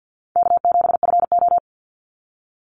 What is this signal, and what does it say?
Morse code, heard as a keyed tone, about 49 wpm, 700 Hz. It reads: K7FO
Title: NA SOTA Chaser Callsigns 50wpm